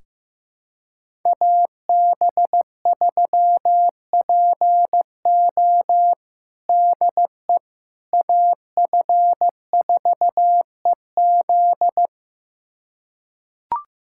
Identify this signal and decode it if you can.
Morse code, heard as a keyed tone, about 15 wpm, 700 Hz. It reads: AB3PO DE AF4EZ
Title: USA Callsign DE Callsign - 5 Characters - 15wpm